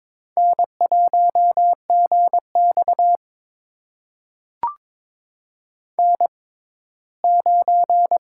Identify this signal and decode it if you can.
Morse code, heard as a keyed tone, about 22 wpm, spaced as for 15 wpm, 700 Hz. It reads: N1GX N9
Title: USA Call Signs - 6x Character Spacing - 22wpm